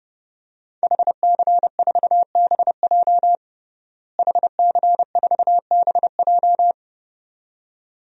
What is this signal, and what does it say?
Morse code, heard as a keyed tone, about 30 wpm, 700 Hz. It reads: HC4BJ HC4BJ
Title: DXCC-100 Call Signs - 5 Characters - Repeated 2x - 30wpm